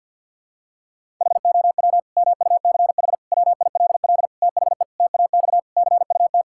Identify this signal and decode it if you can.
Morse code, heard as a keyed tone, about 52 wpm, 700 Hz. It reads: HOW MUCH WILL THE TAX CUT
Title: Sentences from Top 1000 Words - Review Part 5 - 50wpm